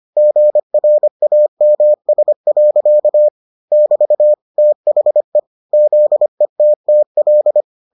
Morse code, heard as a keyed tone, about 25 wpm, 600 Hz. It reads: GRAMS. <BT> THE ZETTL